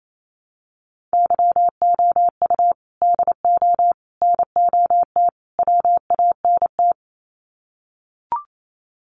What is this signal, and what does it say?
Morse code, heard as a keyed tone, about 28 wpm, 700 Hz. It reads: YOU DO NOT WANT
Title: Sets of 4 Words 28wpm